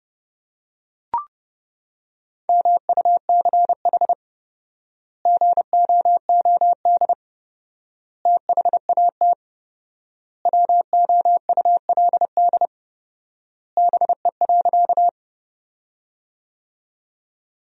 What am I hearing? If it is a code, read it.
Morse code, heard as a keyed tone, about 32 wpm, 700 Hz. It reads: MUCH GOOD THAT WOULD BE.
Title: 4x Word-Spacing - Sentences from Top 200 Words - 30wpm